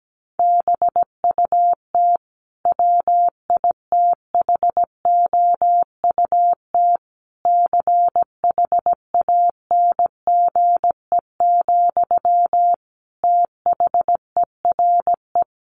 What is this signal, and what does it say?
Morse code, heard as a keyed tone, about 17 wpm, 700 Hz. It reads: BUT WITHOUT CHANGE, THERE
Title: Sentences from Top 1400 Words 17wpm